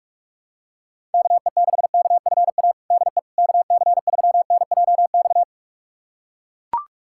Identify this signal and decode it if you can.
Morse code, heard as a keyed tone, about 42 wpm, 700 Hz. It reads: KE6KFA DE KC3NJX
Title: USA Callsign DE Callsign - 6 Characters - 45wpm